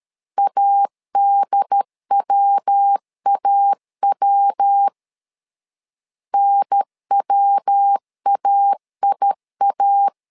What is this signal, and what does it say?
Morse code, heard as a keyed tone, about 13 wpm, 800 Hz. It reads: ADWAW NWAIA
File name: LEZIONI / 3 MWRO / 3 MW 12wpm 1633